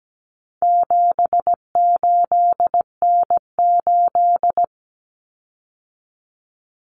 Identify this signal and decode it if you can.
Morse code, heard as a keyed tone, about 17 wpm, 700 Hz. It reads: 78N8
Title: Common Cut Numbers - 4-Digit 17wpm